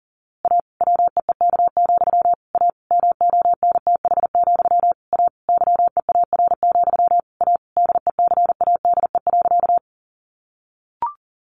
Morse code, heard as a keyed tone, about 40 wpm, 700 Hz. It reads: A WEEK, A MONTH, A YEAR, A DECADE.
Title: A Sound Of Thunder 40wpm